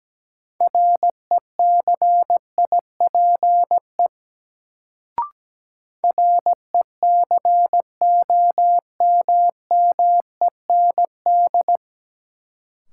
Morse code, heard as a keyed tone, about 17 wpm, 700 Hz. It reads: RECIPE RECOMMEND
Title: Sentences from Top 1200 Words - New Word Drill Alt - 17wpm